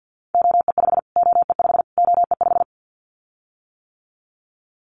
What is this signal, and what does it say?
Morse code, heard as a keyed tone, about 49 wpm, 700 Hz. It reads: OE5 OE5 OE5
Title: Callsign Prefixes for Top 25 DXCC - Repeated 3x - 50wpm